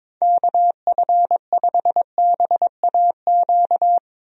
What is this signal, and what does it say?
Morse code, heard as a keyed tone, about 22 wpm, 700 Hz. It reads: KF5BAQ
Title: USA Call Signs - 6 Characters - Repeated 2x - 22wpm